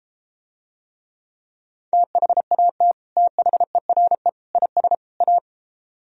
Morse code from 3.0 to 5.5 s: THERE IS A

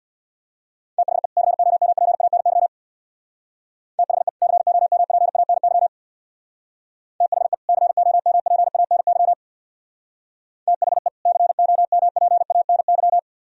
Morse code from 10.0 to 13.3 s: THE COMPANY

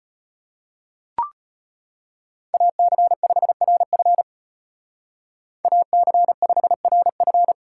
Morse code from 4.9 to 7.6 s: AC5RF

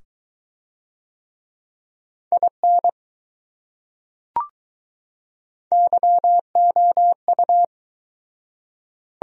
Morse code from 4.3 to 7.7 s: YOU

